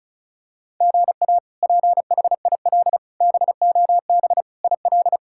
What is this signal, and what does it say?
Morse code, heard as a keyed tone, about 34 wpm, 700 Hz. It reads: GA PHIL BOB IL
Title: K1USN SST QSO Practice - 35wpm